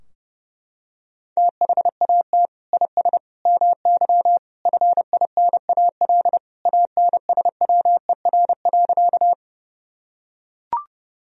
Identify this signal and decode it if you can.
Morse code, heard as a keyed tone, about 30 wpm, 700 Hz. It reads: THAT IS MY FINAL ANSWER.
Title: Sentences from Top 1000 Words - Review Part 2 - 30wpm